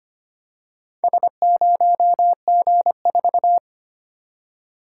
Morse code, heard as a keyed tone, about 25 wpm, 700 Hz. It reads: S0G4
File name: practice / 25wpm / Four Letter-Number Combinations - 25wpm